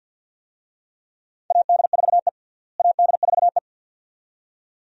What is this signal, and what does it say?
Morse code, heard as a keyed tone, about 45 wpm, 700 Hz. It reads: AD4E AD4E
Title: USA Call Signs - 4 Characters - Repeated 2x - 50wpm